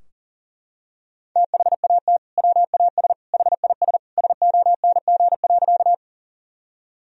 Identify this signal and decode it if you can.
Morse code, heard as a keyed tone, about 40 wpm, 700 Hz. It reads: THAT WAS HIS SONG.
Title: Sentences from Top 1000 Words - Review Part 2 - 40wpm